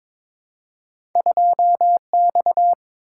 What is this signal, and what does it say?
Morse code, heard as a keyed tone, about 22 wpm, 690 Hz. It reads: 2X
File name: practice / 22wpm / Two Letter-Number Combinations - 22wpm